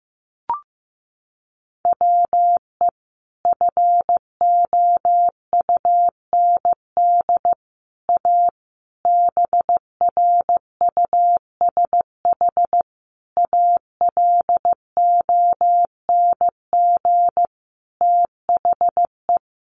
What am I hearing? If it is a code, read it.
Morse code, heard as a keyed tone, about 15 wpm, 700 Hz. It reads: WE FOUND A BRUSH ALONG THE
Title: Sentences from Top 1100 Words 15wpm